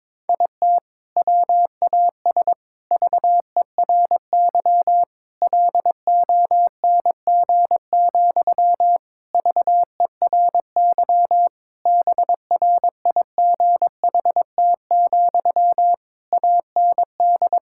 Morse code, heard as a keyed tone, about 22 wpm, 700 Hz. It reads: IT WAS VERY LONG, VERY BRIGHT, AND
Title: Sentences from Top 1600 Words 22wpm